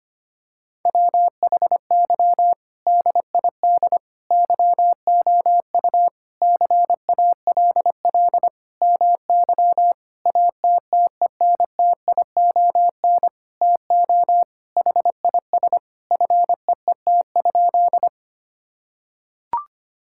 Morse code, heard as a keyed tone, about 25 wpm, 700 Hz. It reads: WHY DID YOU CALL MY ATTENTION TO HIS FEET?
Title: Sentences from Top 1000 Words - Review Part 6 - 25wpm